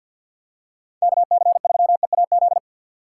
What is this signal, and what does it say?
Morse code, heard as a keyed tone, about 46 wpm, 700 Hz. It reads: KC3EAZ
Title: USA Call Signs - 6 Characters - 4x Character Spacing - 50wpm